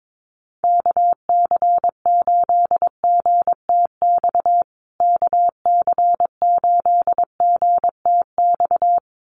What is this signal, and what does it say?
Morse code, heard as a keyed tone, about 22 wpm, 700 Hz. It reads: KC8GTX KC8GTX
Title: USA Call Signs Repeated 2x 22wpm